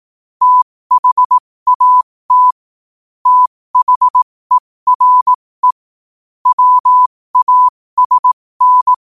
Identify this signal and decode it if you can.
Morse code, heard as a keyed tone, about 18 wpm, spaced as for 15 wpm, 1000 Hz. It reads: THAT THERE WASN
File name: qotdfiles / 15wpm